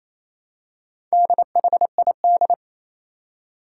Morse code, heard as a keyed tone, about 28 wpm, 700 Hz. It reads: DHID
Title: Four Challenging Letter-Number Combinations 28wpm